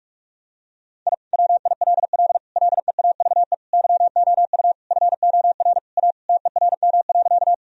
Morse code, heard as a keyed tone, about 46 wpm, 700 Hz. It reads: I WILL LEAVE YOU FOR A TERM.
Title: Sentences from Top 1000 Words - Review Part 4 - 45wpm